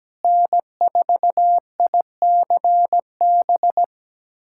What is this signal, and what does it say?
Morse code, heard as a keyed tone, about 17 wpm, 700 Hz. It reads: N4ICB
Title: USA Callsign DE Callsign 17wpm